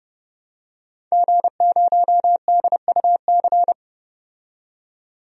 Morse code, heard as a keyed tone, about 30 wpm, 700 Hz. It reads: G0DUC
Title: DXCC-100 Call Signs - 5 Characters - 30wpm